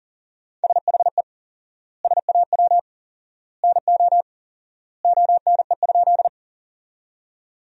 Morse code, heard as a keyed tone, about 40 wpm, 700 Hz. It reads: SHE SAW NO ONE?
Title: 4x Word-Spacing - Sentences from Top 300 Words - 40wpm